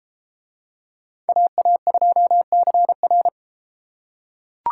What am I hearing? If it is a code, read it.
Morse code, heard as a keyed tone, about 33 wpm, 700 Hz. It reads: AA2CR